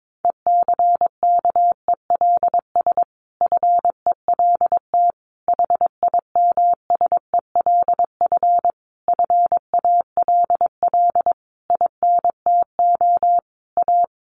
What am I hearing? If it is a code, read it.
Morse code, heard as a keyed tone, about 22 wpm, 700 Hz. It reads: ECKELS FELT HIMSELF FALL INTO A
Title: A Sound Of Thunder 22wpm